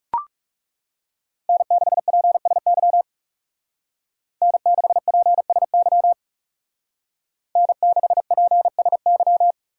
Morse code, heard as a keyed tone, about 35 wpm, 700 Hz. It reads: N6PSY N6PSY N6PSY
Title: Speed Racing - USA 1x3 Call Signs 30wpm